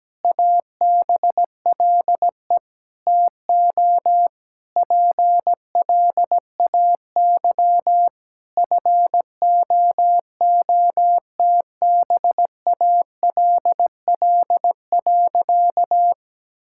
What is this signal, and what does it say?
Morse code, heard as a keyed tone, about 17 wpm, 700 Hz. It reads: ABLE TO PLAY FOOTBALL.
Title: Sentences from Top 1600 Words 17wpm